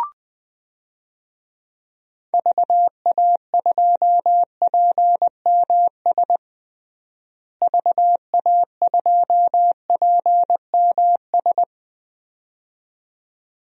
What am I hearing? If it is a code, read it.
Morse code, heard as a keyed tone, about 20 wpm, 700 Hz. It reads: VA2PMS VA2PMS
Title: DXCC-100 Call Signs - 6 Characters - Repeated 2x - 20wpm